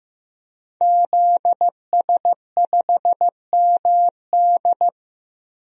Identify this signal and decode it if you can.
Morse code, heard as a keyed tone, about 15 wpm, 700 Hz. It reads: ZS5MD
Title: DXCC-100 Call Signs - 5 Characters - Repeated 2x - 15wpm